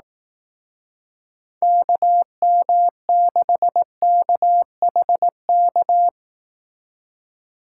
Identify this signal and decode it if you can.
Morse code, heard as a keyed tone, about 18 wpm, 700 Hz. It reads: KM6KHK
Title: Speed Racing - USA 2x3 Call Signs 15wpm